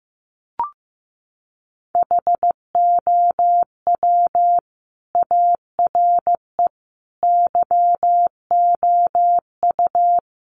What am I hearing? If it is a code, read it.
Morse code, heard as a keyed tone, about 15 wpm, 700 Hz. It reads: HOW ARE YOU
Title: Sets of 5 Words 15wpm